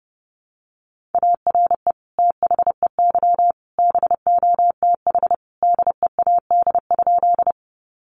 Morse code, heard as a keyed tone, about 30 wpm, 700 Hz. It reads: ARE THEY BOTH DEAD?
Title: Sentences from Top 1500 Words 30wpm